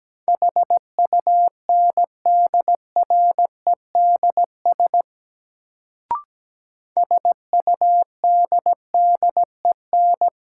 Morse code, heard as a keyed tone, about 17 wpm, 700 Hz. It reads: HUNDREDS SUDDEN
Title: Sets of 4 Words - New Word Drill - 17wpm